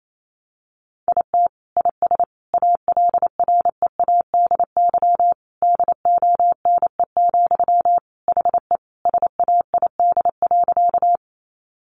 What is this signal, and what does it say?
Morse code, heard as a keyed tone, about 28 wpm, 700 Hz. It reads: IT IS ALREADY DONE, HE SAID.
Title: Sentences from Top 700 Words 28wpm